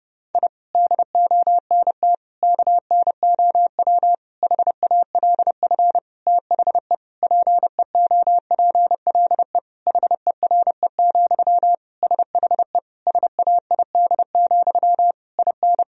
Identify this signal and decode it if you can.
Morse code, heard as a keyed tone, about 30 wpm, 700 Hz. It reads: I DONT KNOW HALF THE PEOPLE HERE, SHE SAID, IN
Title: Sentences from Top 1000 Words - Review Part 1 - 30wpm